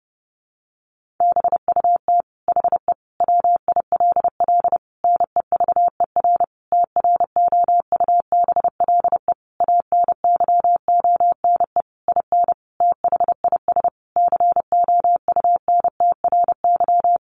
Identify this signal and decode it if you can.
Morse code, heard as a keyed tone, about 30 wpm, 700 Hz. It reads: BUT HE WILL NEVER TROUBLE ANYONE IN THIS COUNTRY
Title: Sentences from Top 1200 Words 30wpm